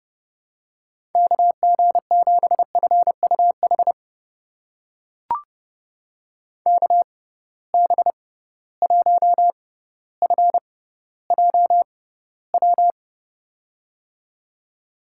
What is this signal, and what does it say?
Morse code, heard as a keyed tone, about 30 wpm, spaced as for 15 wpm, 700 Hz. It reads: KG7FUH KB1FJW
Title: USA Call Signs - 6 Characters - 6x Character Spacing - 30wpm